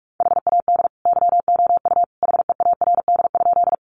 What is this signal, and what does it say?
Morse code, heard as a keyed tone, about 44 wpm, 700 Hz. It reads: HAD YOU HEARD?
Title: Sentences from Top 400 Words 45wpm